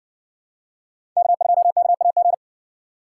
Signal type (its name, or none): Morse code